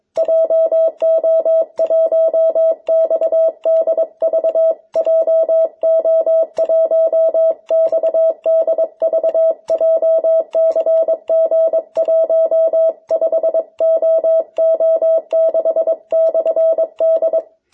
Morse code, heard as a keyed tone, about 22 wpm, 650 Hz. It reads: JO1XDV JO1XDV JCG15OO6/D
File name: jh8cbh / 2021cw